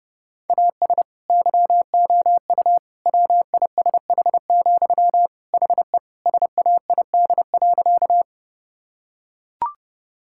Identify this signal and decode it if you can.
Morse code, heard as a keyed tone, about 30 wpm, 700 Hz. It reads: AS YOU WISH, HE SAID.
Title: Sentences from Top 1600 Words 30wpm